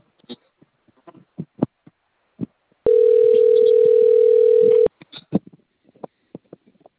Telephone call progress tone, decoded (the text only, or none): ringback tone